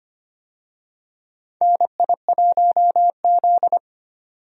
Morse code, heard as a keyed tone, about 25 wpm, 700 Hz. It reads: NI1Z
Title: USA Call Signs - 4 Characters - 25wpm